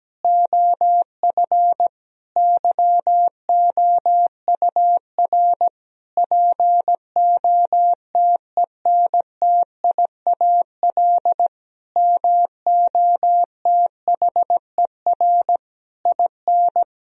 Morse code, heard as a keyed tone, about 17 wpm, 700 Hz. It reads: OF YOUR POTENTIAL MOTHER IN
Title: Sentences from Top 1000 Words - Review Part 4 - 17wpm